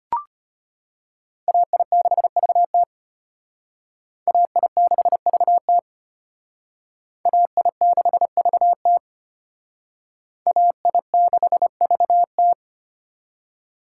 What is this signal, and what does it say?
Morse code, heard as a keyed tone, about 33 wpm, 700 Hz. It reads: AI6VT AI6VT AI6VT AI6VT